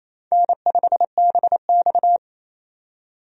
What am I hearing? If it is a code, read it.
Morse code, heard as a keyed tone, about 28 wpm, 700 Hz. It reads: N5BX
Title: USA Call Signs - 4 Characters - 28wpm